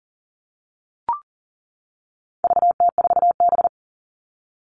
Morse code, heard as a keyed tone, about 40 wpm, 700 Hz. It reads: VT4B
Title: All Cut Numbers - 4-Digit 40wpm